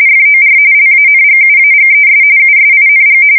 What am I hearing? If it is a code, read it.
RTTY, about 45.45 baud. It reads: FOR LANGUAGES USING
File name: fsk170 / teletype-RTTY